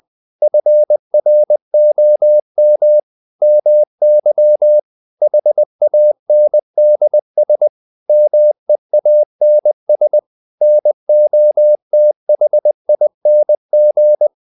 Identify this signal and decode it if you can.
Morse code, heard as a keyed tone, about 20 wpm, 600 Hz. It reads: FROM MY HANDS MEANS NOTHING
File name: the old man and the sea part 2 20wpm